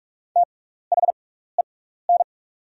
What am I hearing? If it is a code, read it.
Morse code, heard as a keyed tone, about 47 wpm, spaced as for 16 wpm, 690 Hz. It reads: THEN
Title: 6x Character Spacing - Top 200 Words - 45wpm